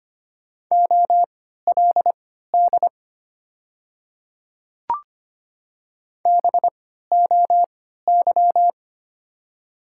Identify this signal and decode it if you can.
Morse code, heard as a keyed tone, about 25 wpm, spaced as for 14 wpm, 700 Hz. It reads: OLD BOY
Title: 3x Character Spacing - Top 200 Words - 25wpm